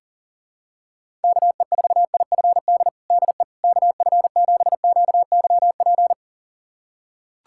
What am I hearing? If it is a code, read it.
Morse code, heard as a keyed tone, about 40 wpm, 700 Hz. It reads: KE4IFD DE KF7QYP
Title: USA Callsign DE Callsign - 6 Characters - 40wpm